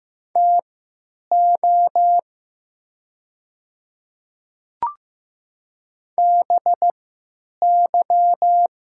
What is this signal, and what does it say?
Morse code, heard as a keyed tone, about 15 wpm, spaced as for 9 wpm, 690 Hz. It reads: TO BY
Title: 3x Character Spacing - Top 100 Words - 15wpm